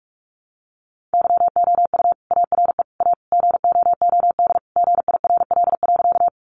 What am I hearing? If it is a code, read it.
Morse code, heard as a keyed tone, about 44 wpm, 700 Hz. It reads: YOU ARE A GOOD GIRL.
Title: Sentences from Top 400 Words 45wpm